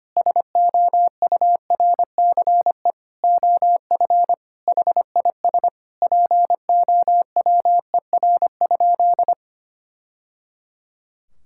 Morse code, heard as a keyed tone, about 25 wpm, 700 Hz. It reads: SOURCE OF HIS POWER?
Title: Sentences from Top 1000 Words - Review Part 5 - 25wpm